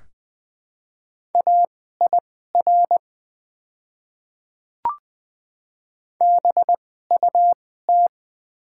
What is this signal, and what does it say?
Morse code, heard as a keyed tone, about 20 wpm, spaced as for 14 wpm, 700 Hz. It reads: AIR BUT
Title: 2x Character Spacing - Top 200 Words - 20wpm